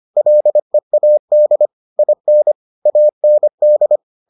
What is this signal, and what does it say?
Morse code, heard as a keyed tone, about 25 wpm, 600 Hz. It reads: LEAD IN AND